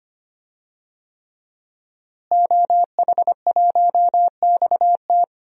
Morse code, heard as a keyed tone, about 25 wpm, 700 Hz. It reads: OH1XT
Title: DXCC-100 Call Signs - 5 Characters - Repeated 3x - 25wpm